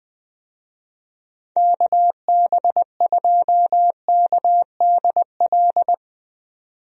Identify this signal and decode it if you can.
Morse code, heard as a keyed tone, about 20 wpm, 700 Hz. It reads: KB2KDL
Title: USA Call Signs - 6 Characters - 20wpm